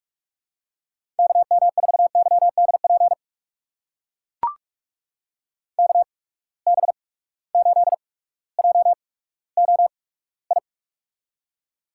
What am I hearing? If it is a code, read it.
Morse code, heard as a keyed tone, about 44 wpm, spaced as for 19 wpm, 700 Hz. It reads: KM4YDP KB8JOI
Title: USA Call Signs - 6 Characters - 8x Character Spacing - 45wpm